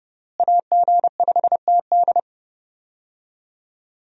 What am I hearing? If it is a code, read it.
Morse code, heard as a keyed tone, about 31 wpm, 700 Hz. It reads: AG5TD
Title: USA Call Signs - 5 Characters - Repeated 2x - 30wpm